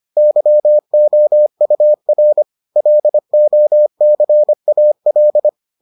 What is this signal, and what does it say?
Morse code, heard as a keyed tone, about 25 wpm, 600 Hz. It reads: YOUR LOCAL